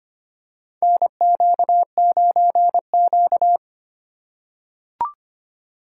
Morse code, heard as a keyed tone, about 25 wpm, 700 Hz. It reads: NQ9Q
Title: USA Call Signs - 4 Characters - 4x Character Spacing - 25wpm